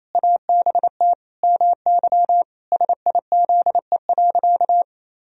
Morse code, heard as a keyed tone, about 27 wpm, 700 Hz. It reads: ABT MY SIZE.
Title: Sentences from Top 400 Words and Abbreviations 28wpm